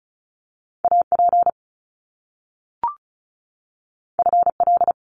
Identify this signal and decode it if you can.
Morse code, heard as a keyed tone, about 36 wpm, 700 Hz. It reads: AP FL
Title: Top Two Letter Combinations in Words 35wpm